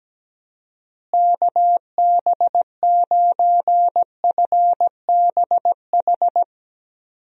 Morse code, heard as a keyed tone, about 17 wpm, 700 Hz. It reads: KB9FBH
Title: USA Call Signs - 6 Characters - Repeated 2x - 17wpm